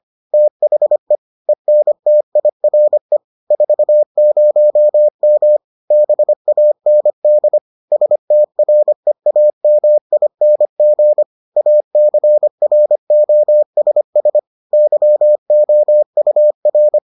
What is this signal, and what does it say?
Morse code, heard as a keyed tone, about 25 wpm, 600 Hz. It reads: THE ENTIRE 40M BAND STREAMING ACROSS YOUR